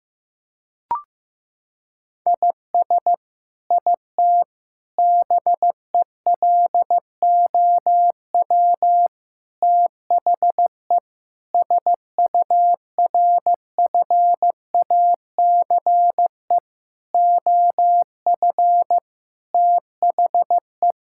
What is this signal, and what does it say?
Morse code, heard as a keyed tone, about 15 wpm, 700 Hz. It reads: IS IT BELOW THE SURFACE OF THE
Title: Sentences from Top 1000 Words - Review Part 5 - 15wpm